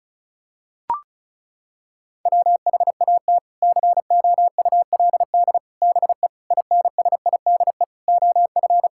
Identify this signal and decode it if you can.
Morse code, heard as a keyed tone, about 36 wpm, 700 Hz. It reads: WHAT COULD BE INSIDE OF
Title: Sentences from Top 1000 Words - Review Part 6 - 35wpm